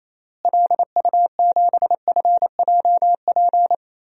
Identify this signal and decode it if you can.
Morse code, heard as a keyed tone, about 28 wpm, 700 Hz. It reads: LU7FJP
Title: DXCC-100 Call Signs - 6 Characters - Repeated 3x - 28wpm